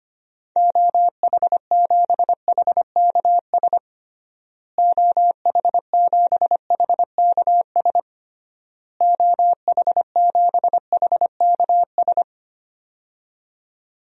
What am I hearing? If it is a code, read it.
Morse code, heard as a keyed tone, about 25 wpm, 690 Hz. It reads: OH7HKS OH7HKS OH7HKS
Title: DXCC-100 Call Signs - 6 Characters - Repeated 3x - 25wpm